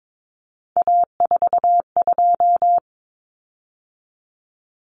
Morse code, heard as a keyed tone, about 22 wpm, 700 Hz. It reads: A42
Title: Common Cut Numbers - 3-Digit 22wpm